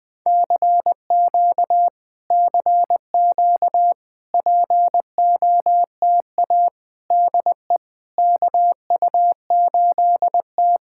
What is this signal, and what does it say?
Morse code, heard as a keyed tone, about 20 wpm, 700 Hz. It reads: CQ CQ POTA DE KU8T
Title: POTA QSO Practice - 20wpm